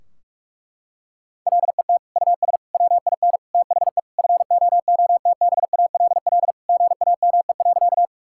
Morse code, heard as a keyed tone, about 46 wpm, 700 Hz. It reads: LET US WIN THE FOOTBALL GAME.
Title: Sentences from Top 1600 Words 45wpm